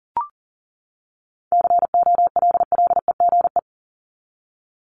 Morse code, heard as a keyed tone, about 40 wpm, 700 Hz. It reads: COLLEGE
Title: Top 1001-1250 Words 40wpm